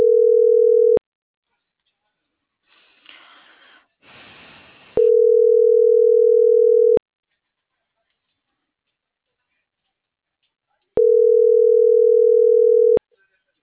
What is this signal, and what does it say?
Ringback tone (North America), 440 + 480 Hz.